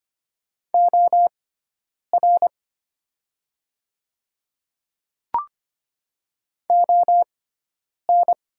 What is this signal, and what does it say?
Morse code, heard as a keyed tone, about 25 wpm, spaced as for 9 wpm, 700 Hz. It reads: OR ON
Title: 6x Character Spacing - Top 100 Words - 25wpm